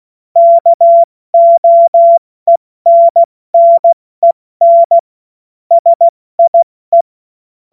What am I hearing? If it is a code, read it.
Morse code, heard as a keyed tone, about 16 wpm, 680 Hz. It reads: KOENNEN SIE
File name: Morsekurs / Witze / Witz 03. Mai 2022 70 F80